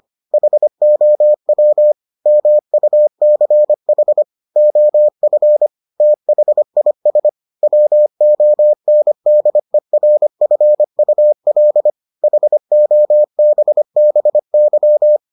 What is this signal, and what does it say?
Morse code, heard as a keyed tone, about 25 wpm, 600 Hz. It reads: HOW MUCH OF THIS WONDERFUL HOBBY